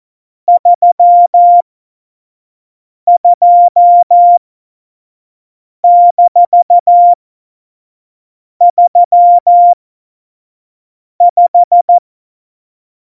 Morse code, heard as a keyed tone, about 14 wpm, spaced as for 5 wpm, 700 Hz. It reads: 32-35
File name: Morsekurs / Witze / Witz 29. September 2020, 25 F70